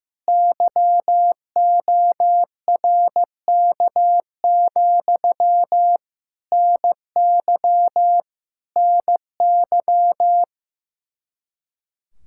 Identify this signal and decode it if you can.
Morse code, heard as a keyed tone, about 15 wpm, 700 Hz. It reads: YORK, NY NY
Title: Speed-Racing Common QSO Phrases Encore - 15wpm